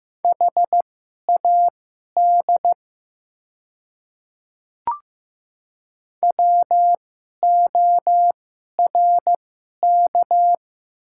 Morse code, heard as a keyed tone, about 15 wpm, spaced as for 11 wpm, 700 Hz. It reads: HAD WORK
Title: 2x Character Spacing - Top 200 Words - 15wpm